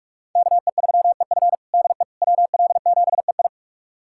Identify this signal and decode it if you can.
Morse code, heard as a keyed tone, about 42 wpm, 700 Hz. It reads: KE3EF DE WL7EI
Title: USA Callsign DE Callsign - 5 Characters - 45wpm